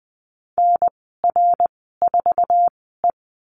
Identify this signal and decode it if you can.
Morse code, heard as a keyed tone, about 20 wpm, spaced as for 14 wpm, 700 Hz. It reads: NR4E
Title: 069 Lesson 12g - Letter d - Call Signs 20wpm